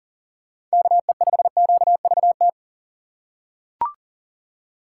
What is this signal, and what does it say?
Morse code, heard as a keyed tone, about 40 wpm, 700 Hz. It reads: KE5QVT
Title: USA Call Signs - 6 Characters - 4x Character Spacing - 40wpm